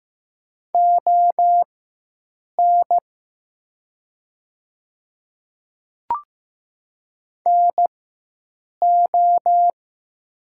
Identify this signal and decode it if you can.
Morse code, heard as a keyed tone, about 15 wpm, spaced as for 7 wpm, 700 Hz. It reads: ON NO
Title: 4x Character Spacing - Top 100 Words - 15wpm